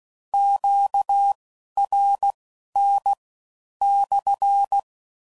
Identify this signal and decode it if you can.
Morse code, heard as a keyed tone, about 16 wpm, spaced as for 11 wpm, 800 Hz. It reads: QRN /
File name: CW Code Groups @15WPM / Q Calls / Q Calls SPSL2